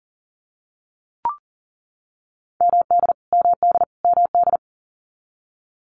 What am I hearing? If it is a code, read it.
Morse code, heard as a keyed tone, about 40 wpm, 700 Hz. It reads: MD MD MD
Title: US States Repeated 3x 40wpm